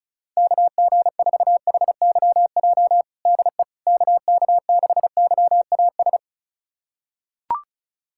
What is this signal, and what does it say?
Morse code, heard as a keyed tone, about 34 wpm, 700 Hz. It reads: KG4HYJ DE KK6YAS
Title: USA Callsign DE Callsign - 6 Characters - 35wpm